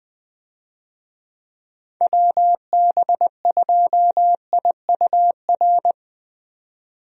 Morse code, heard as a keyed tone, about 20 wpm, 700 Hz. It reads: WB2IUR
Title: USA Call Signs - 6 Characters - 20wpm